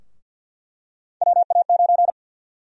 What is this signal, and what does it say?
Morse code, heard as a keyed tone, about 47 wpm, 700 Hz. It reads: WA9
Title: USA Call Sign Prefixes - 50wpm